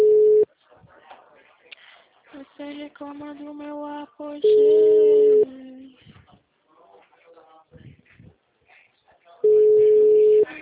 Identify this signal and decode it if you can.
Ringback tone (Europe, 425 Hz), with speech.
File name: out- -2002- - - .36